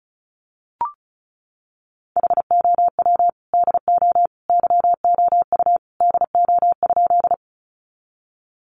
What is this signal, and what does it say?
Morse code, heard as a keyed tone, about 34 wpm, 700 Hz. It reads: HOW DO YOU DO?
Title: Sentences from Top 100 Words 35wpm